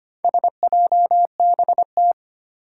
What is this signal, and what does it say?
Morse code, heard as a keyed tone, about 25 wpm, 700 Hz. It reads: SJBT
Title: Four Challenging Letter-Number Combinations 25wpm